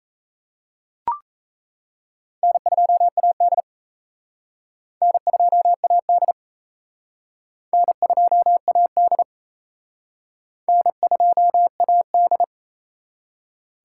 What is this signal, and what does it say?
Morse code, heard as a keyed tone, about 34 wpm, 700 Hz. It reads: N2AD N2AD N2AD N2AD
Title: Speed Racing - USA 1x2 Call Signs 28wpm